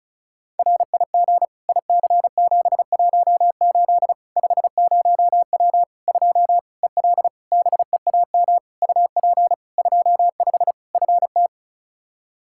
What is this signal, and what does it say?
Morse code, heard as a keyed tone, about 36 wpm, 700 Hz. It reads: RIG IC718 50W 2 EL BEAM UP 25 FT
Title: K1USN SST QSO Practice 35wpm